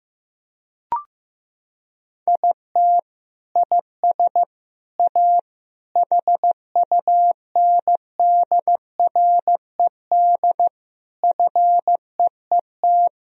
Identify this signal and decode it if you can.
Morse code, heard as a keyed tone, about 16 wpm, 700 Hz. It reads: IT IS A HUNDRED FEET
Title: Sentences from Top 1000 Words - Review Part 2 - 15wpm